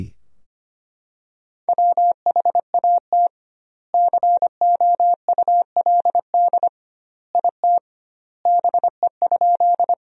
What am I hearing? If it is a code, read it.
Morse code, heard as a keyed tone, about 25 wpm, 700 Hz. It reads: WHAT COULD IT BE?
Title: 2x Word-Spacing - Sentences from Top 100 Words - 25wpm